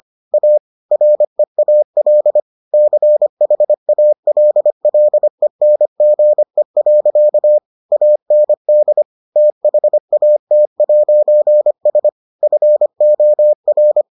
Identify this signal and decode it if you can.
Morse code, heard as a keyed tone, about 25 wpm, 600 Hz. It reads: A REAL CHALLENGE. AND THAT'S FOR